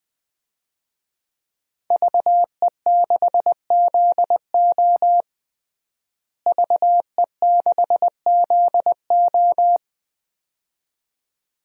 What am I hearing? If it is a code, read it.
Morse code, heard as a keyed tone, about 20 wpm, 700 Hz. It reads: VE6ZO VE6ZO
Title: DXCC-100 Call Signs - 5 Characters - Repeated 2x - 20wpm